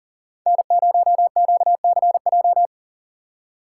Morse code, heard as a keyed tone, about 40 wpm, 700 Hz. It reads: N0QCJ
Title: USA Call Signs - 5 Characters - 4x Character Spacing - 40wpm